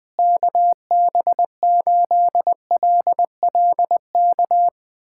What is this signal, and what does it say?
Morse code, heard as a keyed tone, about 21 wpm, 700 Hz. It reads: KB8LLK
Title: USA Callsign DE Callsign 20wpm